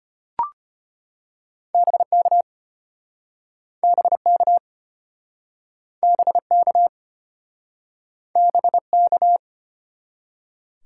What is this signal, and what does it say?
Morse code, heard as a keyed tone, about 33 wpm, 700 Hz. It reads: BK BK BK BK